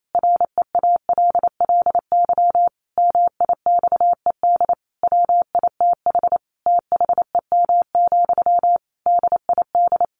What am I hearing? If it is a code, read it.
Morse code, heard as a keyed tone, about 28 wpm, 700 Hz. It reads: REALLY MIXED WITH THEM, DID
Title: Sentences from Top 1500 Words 28wpm